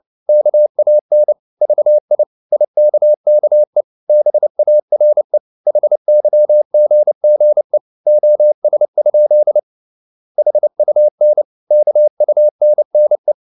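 Morse code, heard as a keyed tone, about 29 wpm, 600 Hz. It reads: KAN VI IKKE BARE HYGGE OS? HUN KUNNE
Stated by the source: Fast (QRQ) Morse at 29 wpm, from CW med Gnister.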